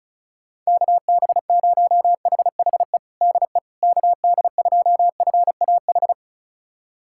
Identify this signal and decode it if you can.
Morse code, heard as a keyed tone, about 34 wpm, 700 Hz. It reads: KB0HHE DE KD2FAH
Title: USA Callsign DE Callsign 35wpm